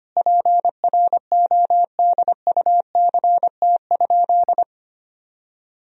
Morse code, heard as a keyed tone, about 25 wpm, 700 Hz. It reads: PRODUCT?
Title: Sentences from Top 1000 Words - Review Part 3 - 25wpm